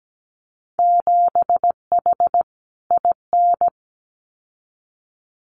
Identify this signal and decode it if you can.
Morse code, heard as a keyed tone, about 17 wpm, 700 Hz. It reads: 7H IN
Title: Winter Field Day Exchanges Encore 17wpm